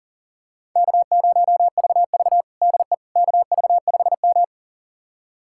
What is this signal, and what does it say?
Morse code, heard as a keyed tone, about 37 wpm, 700 Hz. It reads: K0VV DE KV5M
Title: USA Callsign DE Callsign - 4 Characters - 40wpm